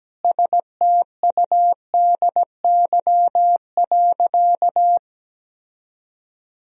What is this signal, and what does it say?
Morse code, heard as a keyed tone, about 17 wpm, 700 Hz. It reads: STUDY.
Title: Sentences from Top 1000 Words - Review Part 1 - 17wpm